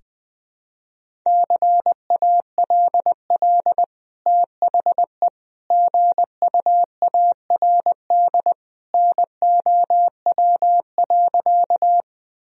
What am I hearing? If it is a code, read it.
Morse code, heard as a keyed tone, about 20 wpm, 700 Hz. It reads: CALL THE GUARD NOW.
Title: Sentences from Top 1000 Words - Review Part 4 - 20wpm